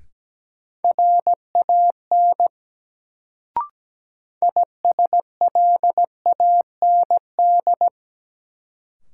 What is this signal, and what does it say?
Morse code, heard as a keyed tone, about 17 wpm, 700 Hz. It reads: RAN ISLAND
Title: Sentences from Top 500 Words - New Word Drill - 17wpm